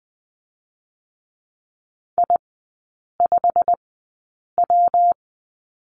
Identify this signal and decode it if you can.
Morse code, heard as a keyed tone, about 20 wpm, 700 Hz. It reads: I 5 W
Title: 114 Lesson 19e - Letter w - Three Characters 20wpm